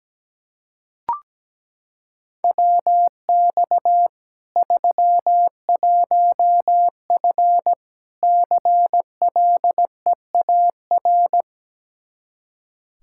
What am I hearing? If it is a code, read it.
Morse code, heard as a keyed tone, about 17 wpm, 700 Hz. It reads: WX 31F CLEAR
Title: Practice QSO - Exchange 2 - Info - 17wpm